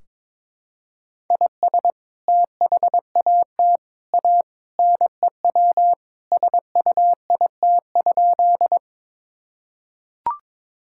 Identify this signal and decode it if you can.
Morse code, heard as a keyed tone, about 22 wpm, 700 Hz. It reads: IS THAT A NEW SUIT?
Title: Sentences from Top 1600 Words 22wpm